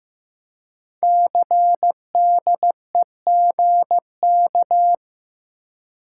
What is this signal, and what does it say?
Morse code, heard as a keyed tone, about 15 wpm, 700 Hz. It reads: CDEGK
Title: Code Groups - 5 Letters - 15wpm